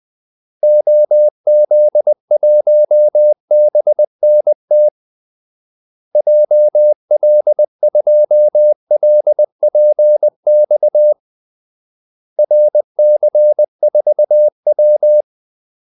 Morse code, heard as a keyed tone, about 20 wpm, 600 Hz. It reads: OZ1BNT JL2LPX RC4W
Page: Kaldesignaler 20 wpm